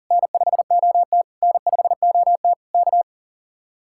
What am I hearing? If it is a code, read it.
Morse code, heard as a keyed tone, about 50 wpm, 700 Hz. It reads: N5OT N5OT K
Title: Speed-Racing Common QSO Phrases - 40wpm